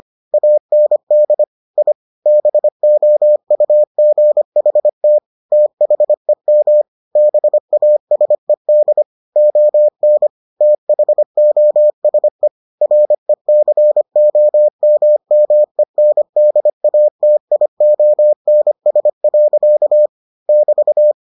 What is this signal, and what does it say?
Morse code, heard as a keyed tone, about 25 wpm, 600 Hz. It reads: AND I BOUGHT THEM BASED ON THOSE RECOMMENDATIONS. <BT>